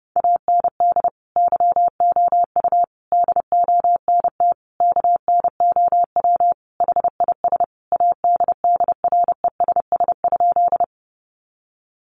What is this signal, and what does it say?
Morse code, heard as a keyed tone, about 28 wpm, 700 Hz. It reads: AND YOU DONT KNOW HIS ADDRESS?
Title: Sentences from Top 1000 Words 30wpm